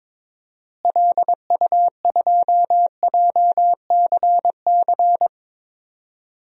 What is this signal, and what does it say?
Morse code, heard as a keyed tone, about 22 wpm, 700 Hz. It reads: LU2JCC
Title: DXCC-100 Call Signs - 6 Characters - Repeated 2x - 22wpm